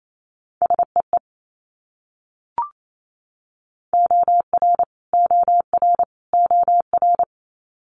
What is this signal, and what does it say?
Morse code, heard as a keyed tone, about 28 wpm, 700 Hz. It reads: SEE OR OR OR
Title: Multi-letter Phonemes Repeated 3x 28wpm